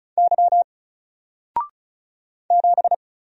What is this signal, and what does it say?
Morse code, heard as a keyed tone, about 35 wpm, 700 Hz. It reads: Y 7
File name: Single-Letter-Number-35wpm